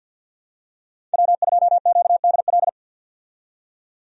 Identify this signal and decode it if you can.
Morse code, heard as a keyed tone, about 51 wpm, 700 Hz. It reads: W2QDL
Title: USA Call Signs - 5 Characters - 50wpm